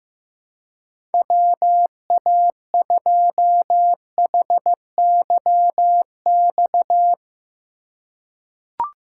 Morse code, heard as a keyed tone, about 15 wpm, 700 Hz. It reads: WA2HYX
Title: USA Call Signs - 6 Characters - 2x Character Spacing - 15wpm